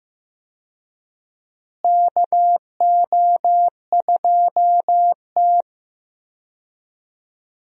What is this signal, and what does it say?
Morse code, heard as a keyed tone, about 15 wpm, 700 Hz. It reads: KO2T
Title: USA Call Signs - 4 Characters - Repeated 3x - 15wpm